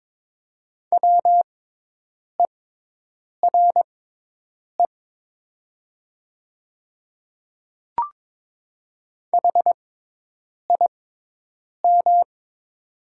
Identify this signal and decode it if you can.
Morse code, heard as a keyed tone, about 21 wpm, spaced as for 8 wpm, 700 Hz. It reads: WERE HIM
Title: 6x Character Spacing - Top 100 Words - 22wpm